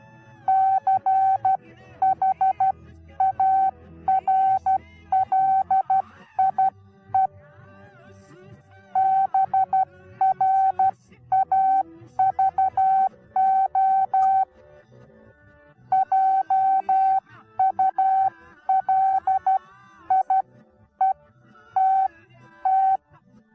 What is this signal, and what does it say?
Morse code, heard as a keyed tone, about 12 wpm, spaced as for 10 wpm, 770 Hz. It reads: CHARLIE BRAVO JULIETT
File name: Onlinelehrgang / morsen / DJ4UF Morsen19